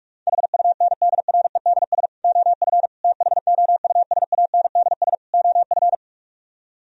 Morse code, heard as a keyed tone, about 49 wpm, 700 Hz. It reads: HUNDREDS OF THOUSANDS OF